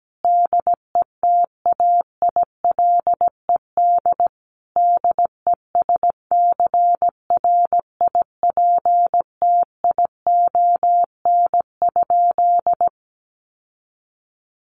Morse code, heard as a keyed tone, about 17 wpm, 700 Hz. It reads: DETAILED DESCRIPTION?
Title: Sentences from Top 1500 Words 17wpm